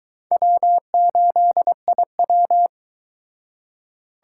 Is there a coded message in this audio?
Yes: Morse code from 0.3 to 2.7 s, heard as a keyed tone, about 22 wpm, 700 Hz. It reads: W8IW